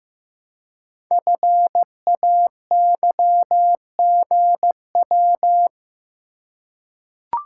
Morse code, heard as a keyed tone, about 15 wpm, 700 Hz. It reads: FAYGW
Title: Code Groups - 5 Letters - 15wpm